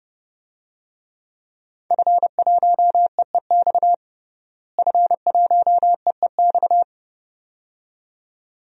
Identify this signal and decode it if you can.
Morse code, heard as a keyed tone, about 30 wpm, 700 Hz. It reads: F1EEX F1EEX
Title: DXCC-100 Call Signs - 5 Characters - Repeated 2x - 30wpm